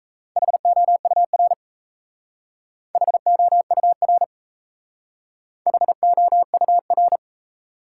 Morse code, heard as a keyed tone, about 36 wpm, 700 Hz. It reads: HOUR HOUR HOUR